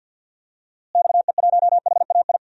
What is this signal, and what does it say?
Morse code, heard as a keyed tone, about 47 wpm, 700 Hz. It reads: XE1HAI
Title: DXCC-100 Call Signs - 6 Characters - Repeated 3x - 50wpm